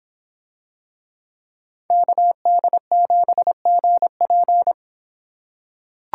Morse code, heard as a keyed tone, about 26 wpm, 700 Hz. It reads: KD7GP